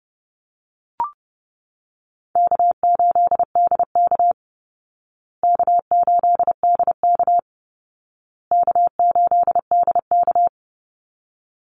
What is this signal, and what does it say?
Morse code, heard as a keyed tone, about 30 wpm, 700 Hz. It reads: K8DK K8DK K8DK
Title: Easy Call Signs Repeated 3x - 30wpm